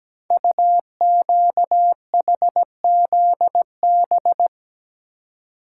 Morse code, heard as a keyed tone, about 17 wpm, 700 Hz. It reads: UQHZB
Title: Code Groups - 5 Letters - 17wpm